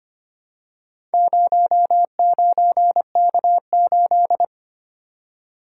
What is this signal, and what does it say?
Morse code, heard as a keyed tone, about 25 wpm, 700 Hz. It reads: 09K8
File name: practice / 25wpm / Four Letter-Number Combinations - 25wpm